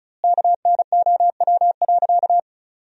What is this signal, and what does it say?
Morse code, heard as a keyed tone, about 34 wpm, 700 Hz. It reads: KNOW.
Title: 2x Word-Spacing - Sentences from Top 400 Words - 35wpm